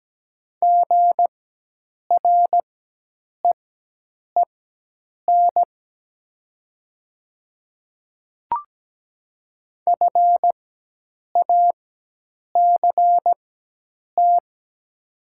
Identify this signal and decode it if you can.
Morse code, heard as a keyed tone, about 17 wpm, spaced as for 8 wpm, 700 Hz. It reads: GREEN FACT
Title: 4x Character Spacing - Top 500 Words - 17wpm